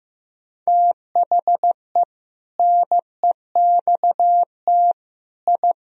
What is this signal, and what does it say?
Morse code, heard as a keyed tone, about 15 wpm, 700 Hz. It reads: THE NEXT I
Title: Sentences from Top 1000 Words - Review Part 3 - 15wpm